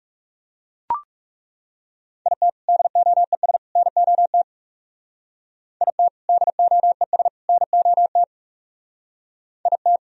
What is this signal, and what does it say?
Morse code, heard as a keyed tone, about 36 wpm, 700 Hz. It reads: IT DOES NOT IT DOES NOT IT
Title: Speed-Racing - Sets of 3 Words 30wpm